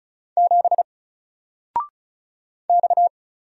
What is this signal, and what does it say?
Morse code, heard as a keyed tone, about 36 wpm, 700 Hz. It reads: 7 X
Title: Single Letter-Number 35wpm